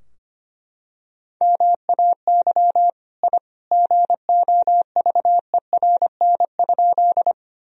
Morse code, heard as a keyed tone, about 25 wpm, 700 Hz. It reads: MAY I GOVERN?
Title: Sentences from Top 1000 Words - Review Part 2 - 25wpm